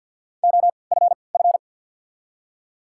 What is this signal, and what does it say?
Morse code, heard as a keyed tone, about 53 wpm, spaced as for 29 wpm, 700 Hz. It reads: OFF
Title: 3x Character Spacing - Top 200 Words - 50wpm